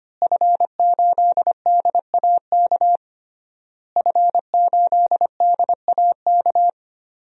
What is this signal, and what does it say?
Morse code, heard as a keyed tone, about 25 wpm, 700 Hz. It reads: F8DAK F8DAK
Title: DXCC-100 Call Signs - 5 Characters - Repeated 3x - 25wpm